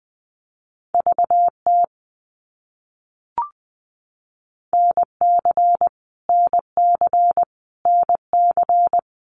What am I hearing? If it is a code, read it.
Morse code, heard as a keyed tone, about 20 wpm, 700 Hz. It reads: VT NC NC NC
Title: US States Repeated 3x 20wpm